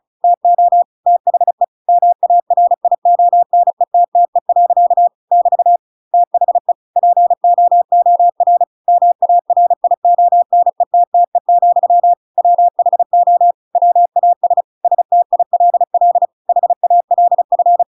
Morse code, heard as a keyed tone, about 34 wpm, 700 Hz. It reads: TO THE MARIONETTE. <BT> THE POOR MARIONETTE, WHO WAS STILL HALF
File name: Pinocchio / Pinocchio ch7 35 35 ik1hge